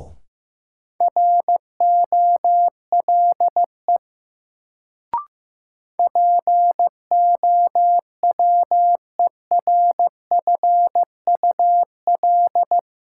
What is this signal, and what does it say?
Morse code, heard as a keyed tone, about 15 wpm, 700 Hz. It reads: ROLE POWERFUL
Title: Sentences from Top 1000 Words - Review Part 6 - New Word Drill - 15wpm